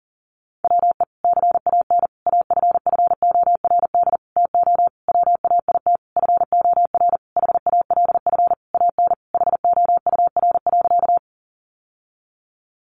Morse code, heard as a keyed tone, about 40 wpm, 700 Hz. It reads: WE CAN AFFORD TO WAIT FOR HALF AN HOUR.
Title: Sentences from Top 1100 Words 40wpm